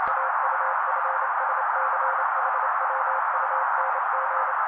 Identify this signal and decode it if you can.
Morse code, heard as a keyed tone, about 28 wpm, 530 Hz. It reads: AUFSCHWUNG
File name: daily / 27wpm noise